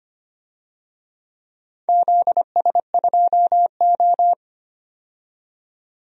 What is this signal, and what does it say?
Morse code, heard as a keyed tone, about 25 wpm, 700 Hz. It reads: ZS2O
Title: DXCC-100 Call Signs - 4 Characters - 25wpm